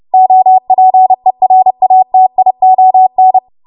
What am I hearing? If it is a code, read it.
Morse code, heard as a keyed tone, about 30 wpm, 750 Hz. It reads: OPERATION
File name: Archive / 30 WPM / 30WPM